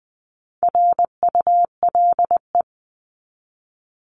Morse code, heard as a keyed tone, about 20 wpm, 700 Hz. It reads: RULE
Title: ES:S Suffix - Top 100 Words with Base Words 20wpm